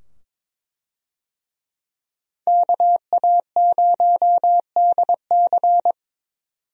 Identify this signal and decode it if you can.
Morse code, heard as a keyed tone, about 22 wpm, 700 Hz. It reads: KA0DC
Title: USA Call Signs - 5 Characters - 22wpm